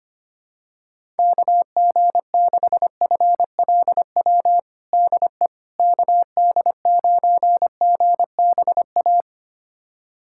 Morse code, heard as a keyed tone, about 25 wpm, 700 Hz. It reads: KG6FLW DE KD9GBA
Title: USA Callsign DE Callsign 25wpm